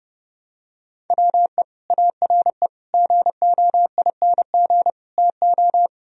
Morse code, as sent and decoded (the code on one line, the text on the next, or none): .-- . / .- .-. / --. --- .. -. --. / - ---
WE ARE GOING TO